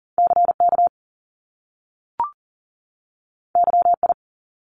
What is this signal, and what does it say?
Morse code, heard as a keyed tone, about 38 wpm, 700 Hz. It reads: CK YI
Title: Top Two Letter Combinations in Words 40wpm